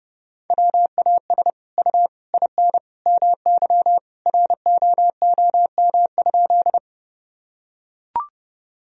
Morse code, heard as a keyed tone, about 30 wpm, 700 Hz. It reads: WAS U IN MY ROOM?
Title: Sentences from Top 300 Words and Abbreviations 30wpm